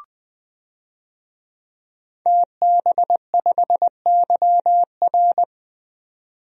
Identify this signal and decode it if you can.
Morse code, heard as a keyed tone, about 20 wpm, 700 Hz. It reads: TB5YR
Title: DXCC-100 Call Signs - 5 Characters - 20wpm